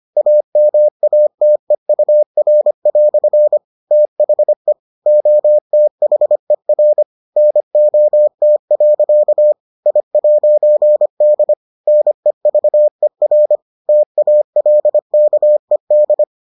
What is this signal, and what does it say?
Morse code, heard as a keyed tone, about 25 wpm, 600 Hz. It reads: AMATEUR" THE OTHER NOT. I'D NEVER TALKED